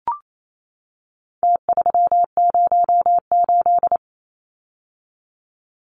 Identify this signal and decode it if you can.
Morse code, heard as a keyed tone, about 28 wpm, 700 Hz. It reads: T308
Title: Numbers - Contest Style - 4-Digit 28wpm